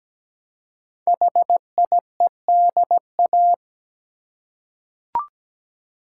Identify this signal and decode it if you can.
Morse code, heard as a keyed tone, about 17 wpm, 700 Hz. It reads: HIEDA
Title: Code Groups - 5 Letters - 17wpm